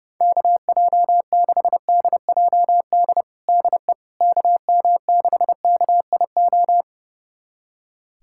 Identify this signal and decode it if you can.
Morse code, heard as a keyed tone, about 32 wpm, 700 Hz. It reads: KJ6DJD DE KM6KIO
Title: USA Callsign DE Callsign 30wpm